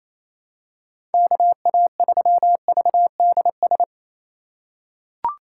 Morse code, heard as a keyed tone, about 27 wpm, 700 Hz. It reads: KA3VDS
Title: USA Call Signs - 6 Characters - 4x Character Spacing - 28wpm